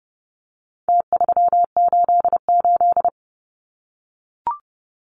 Morse code, heard as a keyed tone, about 30 wpm, 700 Hz. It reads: T388
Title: Numbers - Contest Style - 4-Digit 30wpm